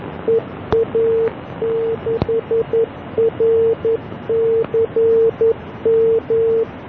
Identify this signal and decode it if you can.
Morse code, heard as a keyed tone, about 11 wpm, 450 Hz. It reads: EA6RCM